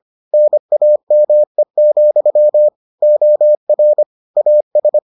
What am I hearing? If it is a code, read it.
Morse code, heard as a keyed tone, about 25 wpm, 600 Hz. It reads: NAME, OR AS